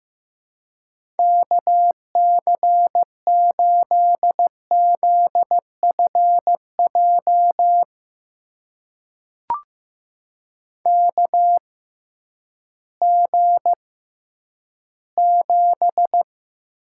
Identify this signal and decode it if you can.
Morse code, heard as a keyed tone, about 15 wpm, spaced as for 10 wpm, 700 Hz. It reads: KC8ZFJ KG7
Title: USA Call Signs - 6 Characters - 6x Character Spacing - 15wpm